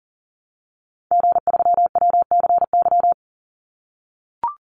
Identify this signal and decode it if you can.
Morse code, heard as a keyed tone, about 40 wpm, 700 Hz. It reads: G3WCY
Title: Call Signs - 40wpm